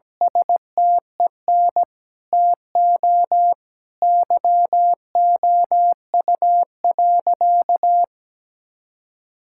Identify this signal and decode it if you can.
Morse code, heard as a keyed tone, about 17 wpm, 700 Hz. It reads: STEN TO YOU.
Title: Sentences from Top 1600 Words 17wpm